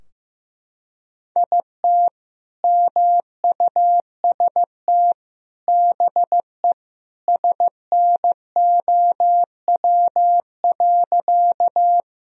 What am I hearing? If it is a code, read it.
Morse code, heard as a keyed tone, about 16 wpm, 700 Hz. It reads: IT MUST BE SNOW.
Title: Sentences from Top 1000 Words - Review Part 3 - 15wpm